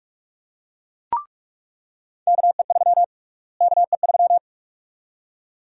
Morse code, heard as a keyed tone, about 46 wpm, 700 Hz. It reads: KE3 KE3
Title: USA Call Sign Prefixes - Repeated 2x - 45wpm